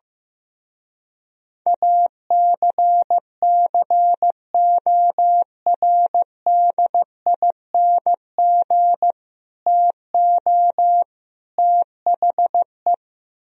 Morse code, heard as a keyed tone, about 16 wpm, 700 Hz. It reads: ACCORDING TO THE
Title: Speed-Racing - Sets of 3 Words 15wpm